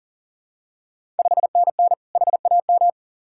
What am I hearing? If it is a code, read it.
Morse code, heard as a keyed tone, about 42 wpm, 690 Hz. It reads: 5NN HAM
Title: Ontario QSO Party Exchange 40wpm